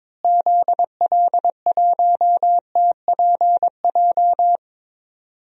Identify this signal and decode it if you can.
Morse code, heard as a keyed tone, about 22 wpm, 700 Hz. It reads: ZL1TPJ
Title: DXCC-100 Call Signs - 6 Characters - Repeated 3x - 22wpm